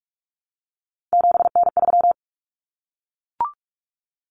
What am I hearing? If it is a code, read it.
Morse code, heard as a keyed tone, about 44 wpm, 700 Hz. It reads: 7N3
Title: Common Cut Numbers - 3-Digit 45wpm